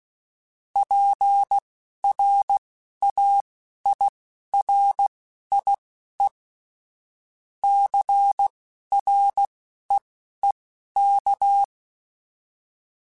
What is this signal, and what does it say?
Morse code, heard as a keyed tone, about 16 wpm, spaced as for 11 wpm, 800 Hz. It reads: PRAIRIE CREEK